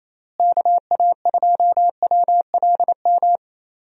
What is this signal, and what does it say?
Morse code, heard as a keyed tone, about 28 wpm, 700 Hz. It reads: KA2WLM
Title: USA Call Signs - 6 Characters - Repeated 3x - 28wpm